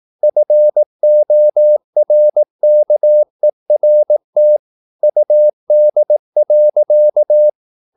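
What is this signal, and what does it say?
Morse code, heard as a keyed tone, about 18 wpm, 600 Hz. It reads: FORKERT UD.
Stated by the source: Slow (QRS) Morse practice at 18 wpm, CW med Gnister.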